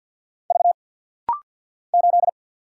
Morse code, heard as a keyed tone, about 50 wpm, 700 Hz. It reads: V 8
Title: Single Letter-Number Alt - Rapid-Fire 50wpm